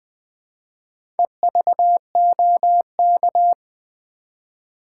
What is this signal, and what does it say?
Morse code, heard as a keyed tone, about 20 wpm, 700 Hz. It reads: EVOK
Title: Build A Word - 1 Letter At A Time - 3 to 5-Letter Words - 20wpm